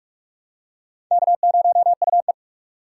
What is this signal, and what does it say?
Morse code, heard as a keyed tone, about 46 wpm, 700 Hz. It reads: K0UE
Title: USA Call Signs - 6x Character Spacing - 45wpm